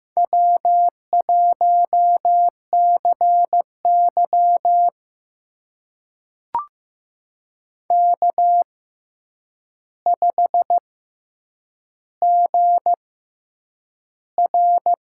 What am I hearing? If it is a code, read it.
Morse code, heard as a keyed tone, about 15 wpm, spaced as for 8 wpm, 700 Hz. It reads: W1CY K5GR
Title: USA Call Signs - 6x Character Spacing - 15wpm